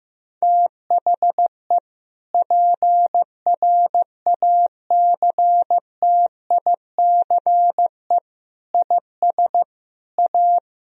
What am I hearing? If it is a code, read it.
Morse code, heard as a keyed tone, about 15 wpm, 700 Hz. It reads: THE PRACTICE IS A
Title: Sentences from Top 1000 Words - Review Part 4 - 15wpm